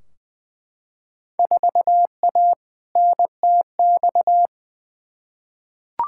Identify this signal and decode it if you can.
Morse code, heard as a keyed tone, about 20 wpm, 700 Hz. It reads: 4A NTX
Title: Field Day Exchanges - 20wpm